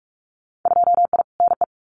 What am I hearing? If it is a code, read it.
Morse code, heard as a keyed tone, about 46 wpm, 700 Hz. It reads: 2I NE
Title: Winter Field Day Exchanges Encore 45wpm